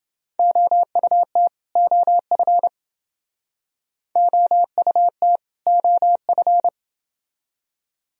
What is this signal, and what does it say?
Morse code, heard as a keyed tone, about 29 wpm, 700 Hz. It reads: OUT OF OUT OF